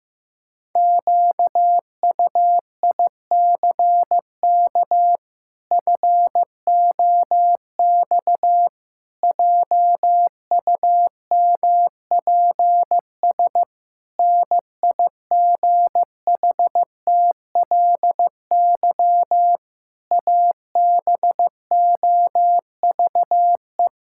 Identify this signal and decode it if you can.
Morse code, heard as a keyed tone, about 15 wpm, 700 Hz. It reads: QUICK FOX JUMPS NIGHTLY ABOVE
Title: Pangram Challenge v2 - 15wpm